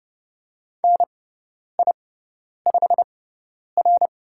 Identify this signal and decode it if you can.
Morse code, heard as a keyed tone, about 30 wpm, spaced as for 10 wpm, 700 Hz. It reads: NI5R
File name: Lesson 16g - Letter c - Call Signs 30wpm